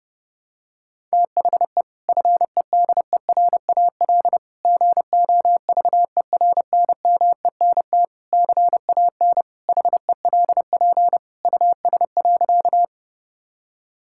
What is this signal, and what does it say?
Morse code, heard as a keyed tone, about 30 wpm, 700 Hz. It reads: THE FEDERAL GOVERNMENT CAN HELP US.
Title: Sentences from Top 1000 Words - Review Part 5 - 30wpm